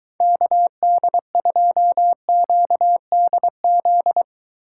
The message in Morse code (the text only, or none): KD2QDZ